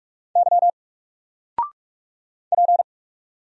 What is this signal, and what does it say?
Morse code, heard as a keyed tone, about 44 wpm, 700 Hz. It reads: Y P
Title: Single Character (Letter:Number:Punctuation:Prosign) 45wpm